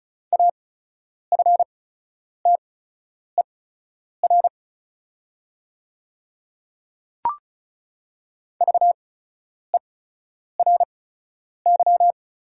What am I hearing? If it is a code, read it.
Morse code, heard as a keyed tone, about 36 wpm, spaced as for 10 wpm, 700 Hz. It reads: AFTER VERY
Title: 8x Character Spacing - Top 300 Words - 35wpm